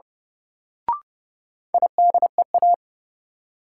Morse code, heard as a keyed tone, about 30 wpm, 700 Hz. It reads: IDEA
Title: Sentences from Top 300 Words - New Word Drill - 30wpm